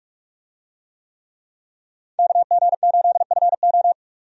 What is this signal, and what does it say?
Morse code, heard as a keyed tone, about 42 wpm, 700 Hz. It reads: KG8FO
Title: USA Call Signs - 5 Characters - 45wpm